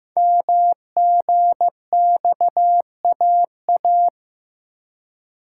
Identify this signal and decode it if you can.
Morse code, heard as a keyed tone, about 15 wpm, 700 Hz. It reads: MGXAA
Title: Code Groups - 5 Letters - 15wpm